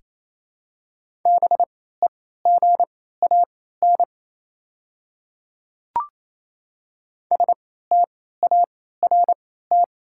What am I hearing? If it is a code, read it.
Morse code, heard as a keyed tone, about 28 wpm, spaced as for 16 wpm, 690 Hz. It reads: BEGAN START
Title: 3x Character Spacing - Top 500 Words - 28wpm